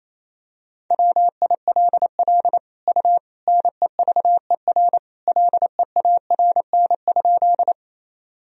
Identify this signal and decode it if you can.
Morse code, heard as a keyed tone, about 28 wpm, 700 Hz. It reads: WILL U NEVER LEARN?
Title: Sentences from Top 300 Words and Abbreviations 28wpm